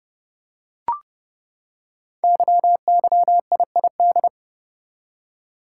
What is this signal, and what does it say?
Morse code, heard as a keyed tone, about 28 wpm, 700 Hz. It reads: YYIID
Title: Code Groups - 5 Letter:Numbers - 30wpm